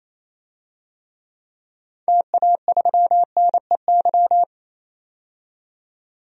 Morse code, heard as a keyed tone, about 28 wpm, 700 Hz. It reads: TA3NEY
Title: DXCC-100 Call Signs - 6 Characters - 28wpm